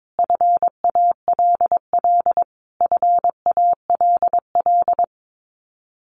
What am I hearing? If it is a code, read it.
Morse code, heard as a keyed tone, about 22 wpm, 700 Hz. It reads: FALL FALL
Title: Top 500 Words Repeated 22wpm